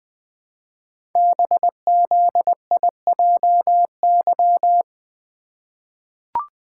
Morse code, heard as a keyed tone, about 20 wpm, 700 Hz. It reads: BZIJY
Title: Code Groups - 5 Letters - 20wpm